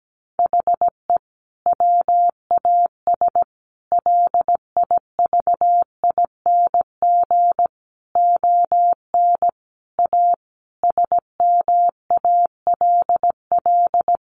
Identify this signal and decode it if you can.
Morse code, heard as a keyed tone, about 17 wpm, 700 Hz. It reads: HE WAS LIVING ON A SMALL
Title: Sentences from Top 1200 Words 17wpm